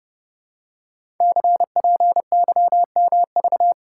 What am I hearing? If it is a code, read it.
Morse code, heard as a keyed tone, about 30 wpm, 700 Hz. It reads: CPYMV
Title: Code Groups - 5 Letter:Numbers - 30wpm